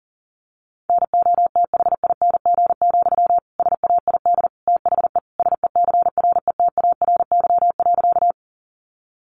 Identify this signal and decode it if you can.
Morse code, heard as a keyed tone, about 37 wpm, 700 Hz. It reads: NOTHING, SAID THE SECRETARY.
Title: Sentences from Top 1300 Words 40wpm